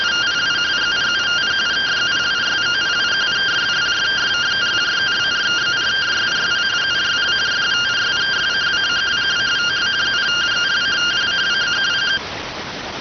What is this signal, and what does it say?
Two-tone teleprinter signal, about 170 Hz shift.